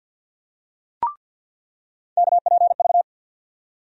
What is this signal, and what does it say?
Morse code, heard as a keyed tone, about 49 wpm, 700 Hz. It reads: KPV
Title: USA Call Sign Postfixes - 3 Letters 50wpm